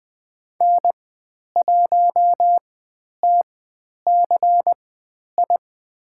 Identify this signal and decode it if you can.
Morse code, heard as a keyed tone, about 20 wpm, spaced as for 10 wpm, 700 Hz. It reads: N1TCI
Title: 095 Lesson 16g - Letter c - Call Signs 20wpm